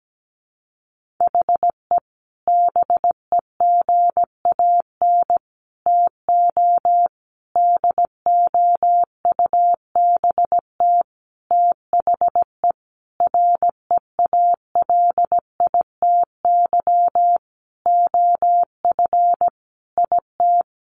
Morse code, heard as a keyed tone, about 17 wpm, 700 Hz. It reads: HE BEGAN TO DOUBT THE REALITY OF IT
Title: Sentences from Top 1500 Words 17wpm